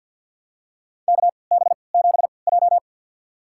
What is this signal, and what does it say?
Morse code, heard as a keyed tone, about 47 wpm, spaced as for 28 wpm, 700 Hz. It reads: KB7J
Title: USA Call Signs - 4 Characters - 3x Character Spacing - 50wpm